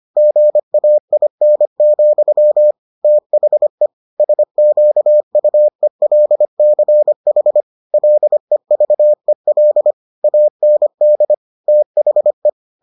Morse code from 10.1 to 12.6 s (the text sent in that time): AND THE